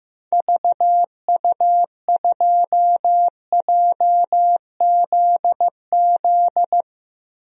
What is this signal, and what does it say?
Morse code, heard as a keyed tone, about 15 wpm, 700 Hz. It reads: VU2JZZ
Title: DXCC-100 Call Signs - 6 Characters - 15wpm